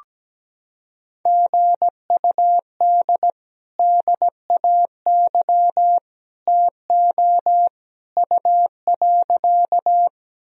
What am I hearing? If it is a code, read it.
Morse code, heard as a keyed tone, about 17 wpm, 700 Hz. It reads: GUD DAY TO U.
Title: Sentences from Top 200 Words and Abbreviations 17wpm